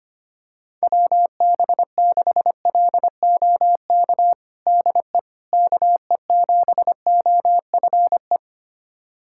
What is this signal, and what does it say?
Morse code, heard as a keyed tone, about 25 wpm, 700 Hz. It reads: WB6LOK DE KE7OFE
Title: USA Callsign DE Callsign - 6 Characters - 25wpm